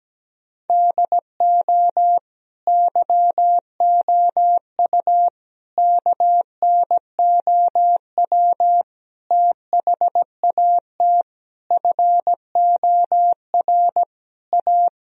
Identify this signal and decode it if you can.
Morse code, heard as a keyed tone, about 17 wpm, 700 Hz. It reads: DO YOU KNOW THAT FOR A
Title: Sentences from Top 1000 Words - Review Part 3 - 17wpm